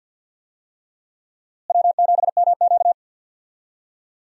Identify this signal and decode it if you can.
Morse code, heard as a keyed tone, about 51 wpm, 700 Hz. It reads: W7MQ
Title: USA Call Signs - 4 Characters - Repeated 3x - 50wpm